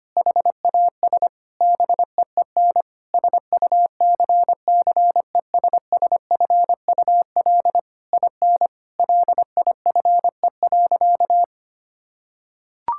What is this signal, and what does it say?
Morse code, heard as a keyed tone, about 25 wpm, 700 Hz. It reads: HAS BEEN SUCCESSFUL IN LIFE.
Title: Sentences from Top 1000 Words - Review Part 6 - 25wpm